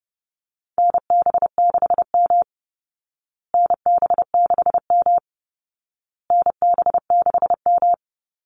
Morse code, heard as a keyed tone, about 30 wpm, 700 Hz. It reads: NB6M NB6M NB6M
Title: Call Signs Repeated 2x - 30wpm